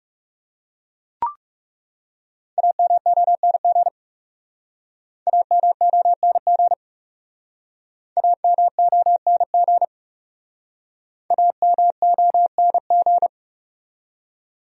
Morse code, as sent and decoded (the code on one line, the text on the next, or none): .- -- --- -. --. / .- -- --- -. --. / .- -- --- -. --. / .- -- --- -. --.
AMONG AMONG AMONG AMONG